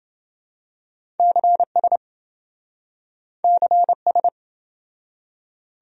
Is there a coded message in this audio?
Yes: Morse code from 1.2 to 4.3 s, heard as a keyed tone, about 28 wpm, 700 Hz. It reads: CS CS